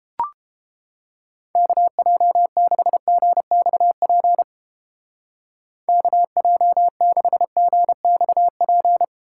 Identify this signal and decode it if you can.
Morse code, heard as a keyed tone, about 29 wpm, 700 Hz. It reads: KJ6GXP KJ6GXP